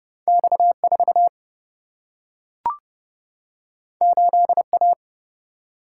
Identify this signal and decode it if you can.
Morse code, heard as a keyed tone, about 30 wpm, 700 Hz. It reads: X 4 8 A
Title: Single-Letter Single-Number 30wpm